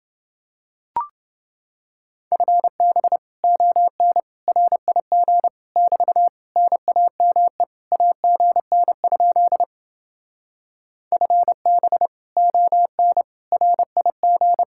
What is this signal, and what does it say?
Morse code, heard as a keyed tone, about 28 wpm, 700 Hz. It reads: FB ON RIG <BT> NAME AGN? FB ON RIG